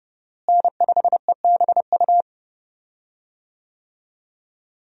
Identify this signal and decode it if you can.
Morse code, heard as a keyed tone, about 32 wpm, 700 Hz. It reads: N5EBU
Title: USA Call Signs - 5 Characters - Repeated 3x - 30wpm